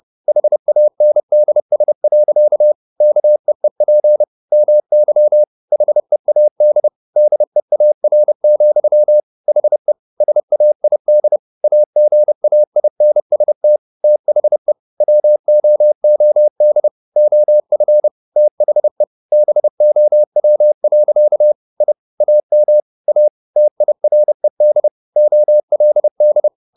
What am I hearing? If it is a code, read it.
Morse code, heard as a keyed tone, about 30 wpm, 600 Hz. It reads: HANDS. KEEP MY HEAD DEAR, HE SAID AGAINST THE WOOD OF THE BOW. I AM A TIRED OLD